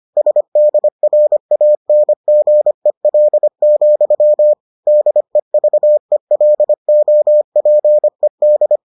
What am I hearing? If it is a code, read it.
Morse code, heard as a keyed tone, about 25 wpm, 600 Hz. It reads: SDRANGEL, DEVELOPED